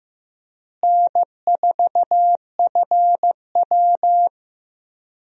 Morse code, heard as a keyed tone, about 16 wpm, 700 Hz. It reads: N4FW
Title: USA Call Signs - 4 Characters - Repeated 3x - 15wpm